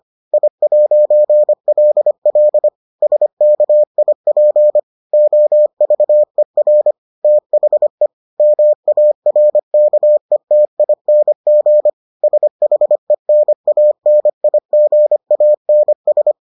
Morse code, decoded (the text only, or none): I'LL SKIP OVER THE MARKETING SHENANIGANS